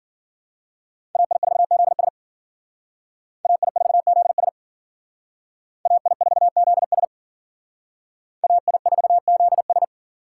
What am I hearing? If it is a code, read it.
Morse code, heard as a keyed tone, about 51 wpm, 700 Hz. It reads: AI4ZS AI4ZS AI4ZS AI4ZS